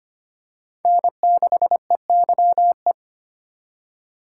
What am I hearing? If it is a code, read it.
Morse code, heard as a keyed tone, about 25 wpm, 700 Hz. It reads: N6EYE
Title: USA Call Signs - 5 Characters - 2x Character Spacing - 25wpm